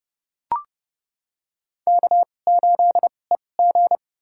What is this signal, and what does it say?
Morse code, heard as a keyed tone, about 30 wpm, spaced as for 22 wpm, 700 Hz. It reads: K8EG
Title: USA Call Signs - 4 Characters - 2x Character Spacing - 30wpm